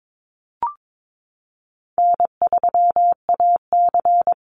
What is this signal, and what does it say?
Morse code, heard as a keyed tone, about 22 wpm, 700 Hz. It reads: N3AC
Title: Call Signs Repeated 2x - 22wpm